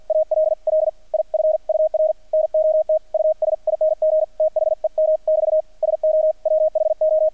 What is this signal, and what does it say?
Morse code, heard as a keyed tone, about 48 wpm, 630 Hz, with radio noise. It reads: ALL IF UR NOT USING THEM <BT> SO WHO
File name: archived / CRSnet